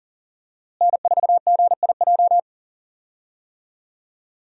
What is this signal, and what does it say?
Morse code, heard as a keyed tone, about 40 wpm, 700 Hz. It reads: N4GIJ
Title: USA Call Signs - 5 Characters - Repeated 2x - 40wpm